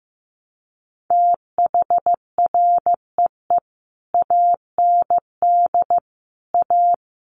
Morse code, heard as a keyed tone, about 15 wpm, 700 Hz. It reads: THREE AND A
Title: Sets of 5 Words Encore2 15wpm